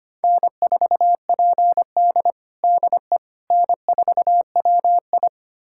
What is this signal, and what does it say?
Morse code, heard as a keyed tone, about 25 wpm, 700 Hz. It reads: N4PD DE N4WI
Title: USA Callsign DE Callsign - 4 Characters - 25wpm